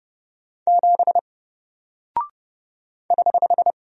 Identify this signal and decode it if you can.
Morse code, heard as a keyed tone, about 30 wpm, 700 Hz. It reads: 7 <HH>
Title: Single Character (Letter:Number:Punctuation:Prosign) Alt 30wpm